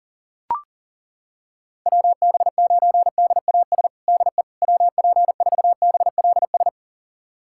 Morse code, heard as a keyed tone, about 40 wpm, 700 Hz. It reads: WB9DAS DE WP4BLS
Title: USA Callsign DE Callsign - 6 Characters - 40wpm